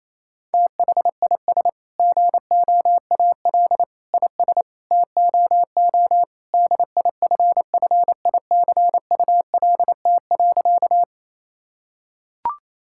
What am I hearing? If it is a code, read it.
Morse code, heard as a keyed tone, about 28 wpm, 700 Hz. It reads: THIS GOAL IS TOO DIFFICULT.
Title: Sentences from Top 1000 Words - Review Part 5 - 28wpm